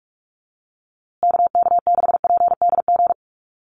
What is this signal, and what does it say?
Morse code, heard as a keyed tone, about 46 wpm, 690 Hz. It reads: KK6PDG
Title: USA Call Signs Repeated 3x 45wpm